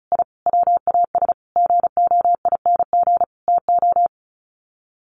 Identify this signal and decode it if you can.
Morse code, heard as a keyed tone, about 34 wpm, 700 Hz. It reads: I WAS GOING TO
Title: Sets of 5 Words Encore2 35wpm